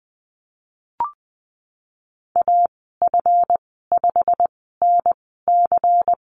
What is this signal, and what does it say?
Morse code, heard as a keyed tone, about 20 wpm, spaced as for 15 wpm, 700 Hz. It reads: AF5NC
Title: 147 Lesson 24g - Letter y - Call Signs 20wpm